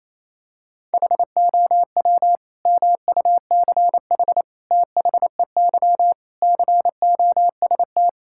Morse code, heard as a keyed tone, about 28 wpm, 700 Hz. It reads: HOW MUCH THEY COST
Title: Sentences from Top 1000 Words - Review Part 5 - 28wpm